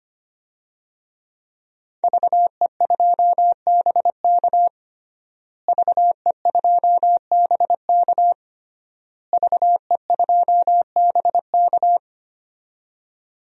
Morse code, heard as a keyed tone, about 25 wpm, 700 Hz. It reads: VE2BK VE2BK VE2BK
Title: DXCC-100 Call Signs - 5 Characters - Repeated 3x - 25wpm